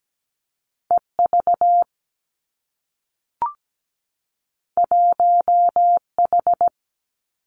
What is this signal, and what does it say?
Morse code, heard as a keyed tone, about 17 wpm, 700 Hz. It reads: E V 1 H
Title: Two Challenging Letter-Number Combinations 17wpm